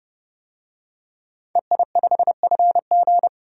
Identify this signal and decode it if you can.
Morse code, heard as a keyed tone, about 30 wpm, 700 Hz. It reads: EI5FG
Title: DXCC-100 Call Signs - 5 Characters - 30wpm